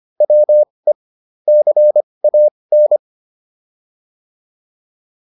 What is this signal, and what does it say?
Morse code, heard as a keyed tone, about 25 wpm, spaced as for 20 wpm, 600 Hz. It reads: WE CAN